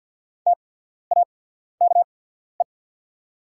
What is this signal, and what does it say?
Morse code, heard as a keyed tone, about 48 wpm, spaced as for 14 wpm, 700 Hz. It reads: TAKE
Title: 8x Character Spacing - Top 300 Words - 50wpm